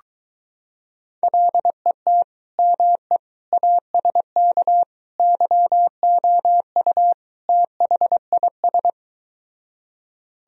Morse code, heard as a keyed tone, about 23 wpm, 700 Hz. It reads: LET ME ASK YOU THIS